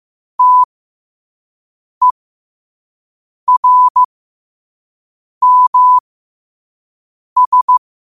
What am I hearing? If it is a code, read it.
Morse code, heard as a keyed tone, about 15 wpm, spaced as for 5 wpm, 1000 Hz. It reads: TERMS
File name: qotdfiles / 5wpm